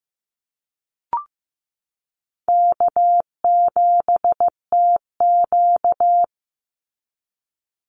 Morse code, heard as a keyed tone, about 15 wpm, 700 Hz. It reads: K7TQ
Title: NA SOTA Chaser Callsigns 15wpm